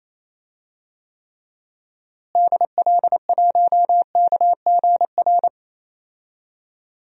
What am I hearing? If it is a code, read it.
Morse code, heard as a keyed tone, about 28 wpm, 700 Hz. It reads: DL1KGR
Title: DXCC-100 Call Signs - 6 Characters - Repeated 3x - 28wpm